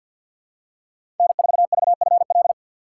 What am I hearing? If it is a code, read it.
Morse code, heard as a keyed tone, about 49 wpm, 700 Hz. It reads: N4VFL
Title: USA Call Signs - 5 Characters - 6x Character Spacing - 50wpm